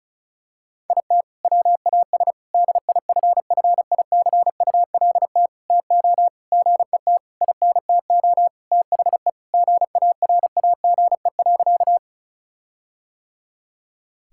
Morse code, heard as a keyed tone, about 36 wpm, 700 Hz. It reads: IT WAS DIFFICULT TO GET INTO THE GARAGE.
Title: Sentences from Top 1600 Words 35wpm